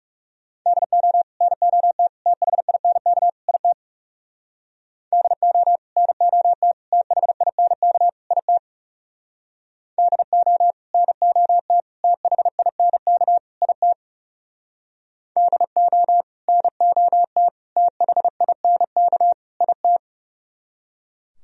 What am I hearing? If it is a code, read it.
Morse code, heard as a keyed tone, about 37 wpm, 700 Hz. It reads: DO NOT THINK IT DO NOT THINK IT DO NOT THINK IT DO NOT THINK IT
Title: Speed-Racing - Sets of 4 Words 30wpm